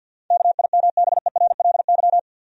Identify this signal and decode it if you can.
Morse code, heard as a keyed tone, about 53 wpm, 700 Hz. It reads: KIMBERLY
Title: USA Common Names - 50wpm